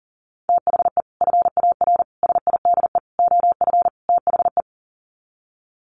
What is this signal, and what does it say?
Morse code, heard as a keyed tone, about 40 wpm, 700 Hz. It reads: THE FAR SIDE OF THE
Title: Sets of 5 Words 40wpm